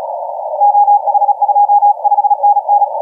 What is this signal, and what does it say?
Morse code, heard as a keyed tone, about 34 wpm, 820 Hz, with radio noise. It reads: OH25NN
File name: TO2TT 1438UTC 15m